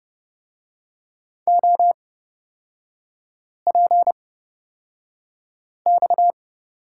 Morse code, heard as a keyed tone, about 30 wpm, spaced as for 5 wpm, 700 Hz. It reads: OPX
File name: Lesson 37e - Letter x - Three Characters 30wpm